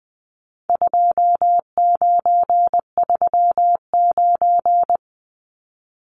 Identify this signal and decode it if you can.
Morse code, heard as a keyed tone, about 20 wpm, 700 Hz. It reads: 2939
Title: Numbers - 4-Digit 20wpm